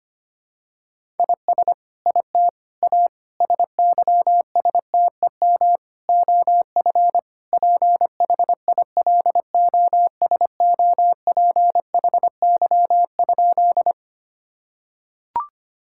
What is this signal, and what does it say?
Morse code, heard as a keyed tone, about 25 wpm, 700 Hz. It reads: IS IT A SYSTEM OF PHILOSOPHY?
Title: Sentences from Top 1000 Words - Review Part 6 - 25wpm